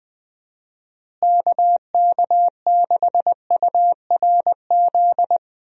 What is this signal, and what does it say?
Morse code, heard as a keyed tone, about 20 wpm, 700 Hz. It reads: KK6URZ
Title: USA Call Signs - 6 Characters - Repeated 3x - 20wpm